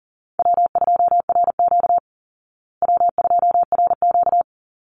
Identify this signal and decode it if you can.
Morse code, heard as a keyed tone, about 40 wpm, 700 Hz. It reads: W2RQ W2RQ
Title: Easy Call Signs Repeated 2x - 40wpm